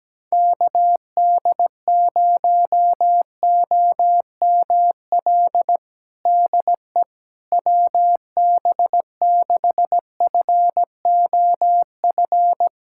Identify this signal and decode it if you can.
Morse code, heard as a keyed tone, about 17 wpm, 700 Hz. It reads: KD0OML DE WB6FOF
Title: USA Callsign DE Callsign 17wpm